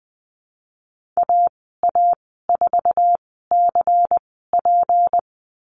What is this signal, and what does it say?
Morse code, heard as a keyed tone, about 20 wpm, spaced as for 15 wpm, 700 Hz. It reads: AA4CP
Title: 173 Lesson 28g - Number 9 - Call Signs 20wpm